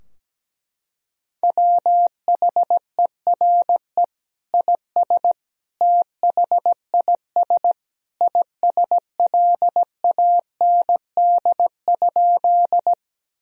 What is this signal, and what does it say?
Morse code, heard as a keyed tone, about 17 wpm, 700 Hz. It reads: WHERE IS THIS ISLAND?
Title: Sentences from Top 1000 Words - Review Part 3 - 17wpm